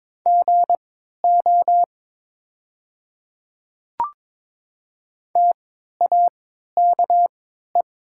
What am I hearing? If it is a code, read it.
Morse code, heard as a keyed tone, about 22 wpm, spaced as for 13 wpm, 700 Hz. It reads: GO TAKE
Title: 3x Character Spacing - Top 300 Words - 22wpm